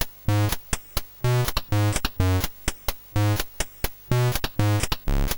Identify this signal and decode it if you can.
There is no code.